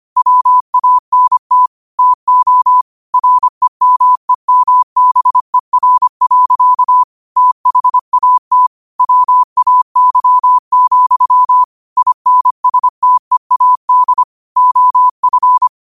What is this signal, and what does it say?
Morse code, heard as a keyed tone, about 25 wpm, 1000 Hz. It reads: WANT TO REMEMBER. THAT WAY, INSTEAD OF